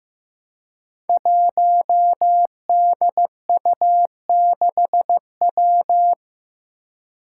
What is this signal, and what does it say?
Morse code, heard as a keyed tone, about 15 wpm, 700 Hz. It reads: 1DU6W
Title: Five Challenging Letter-Number Combinations 15wpm